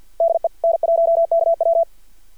Morse code, heard as a keyed tone, about 51 wpm, 670 Hz. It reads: DE N1KW
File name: archived / CRSnet